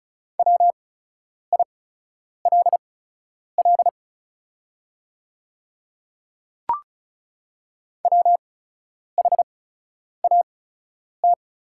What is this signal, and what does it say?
Morse code, heard as a keyed tone, about 36 wpm, spaced as for 10 wpm, 690 Hz. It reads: WILL WHAT
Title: 8x Character Spacing - Top 100 Words - 35wpm